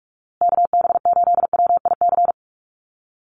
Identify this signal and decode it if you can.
Morse code, heard as a keyed tone, about 44 wpm, 700 Hz. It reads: KB8WIC
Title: USA Call Signs Repeated 2x 45wpm